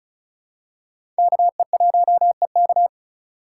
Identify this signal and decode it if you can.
Morse code, heard as a keyed tone, about 35 wpm, 700 Hz. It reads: KE1EK
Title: USA Call Signs - 5 Characters - 2x Character Spacing - 35wpm